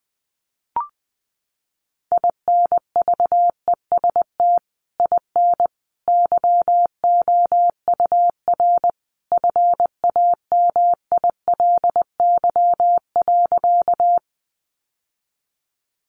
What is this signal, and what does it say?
Morse code, heard as a keyed tone, about 20 wpm, 700 Hz. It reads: INVEST IN YOUR FAMILY.
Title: Sentences from Top 1100 Words 20wpm